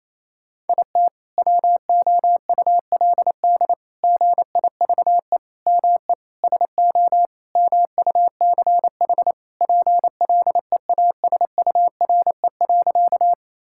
Morse code, heard as a keyed tone, about 28 wpm, 700 Hz. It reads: IT WOULD GIVE ME SO MUCH PLEASURE.
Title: Sentences from Top 1600 Words 28wpm